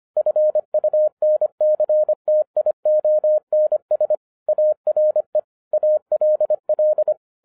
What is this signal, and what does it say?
Morse code, heard as a keyed tone, about 25 wpm, 610 Hz. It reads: FUNCTIONS ARE ALL
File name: 2011-10-mp3 / n09-CW TO PS2 CONVERTER 25wpm